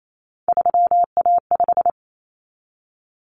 Morse code, heard as a keyed tone, about 28 wpm, 700 Hz. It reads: 3A5
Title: Common Cut Numbers - 3-Digit 28wpm